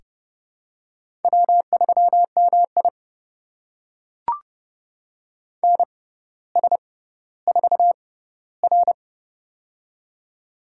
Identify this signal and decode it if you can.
Morse code, heard as a keyed tone, about 30 wpm, spaced as for 15 wpm, 700 Hz. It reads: W3MI NS4R
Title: USA Call Signs - 4 Characters - 6x Character Spacing - 30wpm